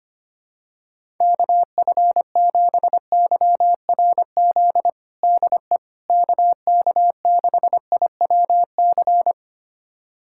Morse code, heard as a keyed tone, about 25 wpm, 700 Hz. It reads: KF7YRZ DE KK6IWC
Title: USA Callsign DE Callsign - 6 Characters - 25wpm